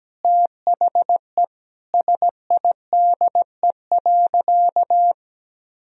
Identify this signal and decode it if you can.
Morse code, heard as a keyed tone, about 17 wpm, 700 Hz. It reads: THE SIDE.
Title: Sentences from Top 1000 Words - Review Part 4 - 17wpm